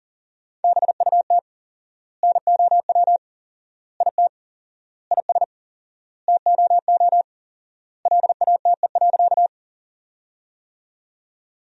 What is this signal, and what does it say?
Morse code, heard as a keyed tone, about 37 wpm, 700 Hz. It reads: BUT NOW IT IS TOO LATE.
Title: 4x Word-Spacing - Sentences from Top 300 Words - 40wpm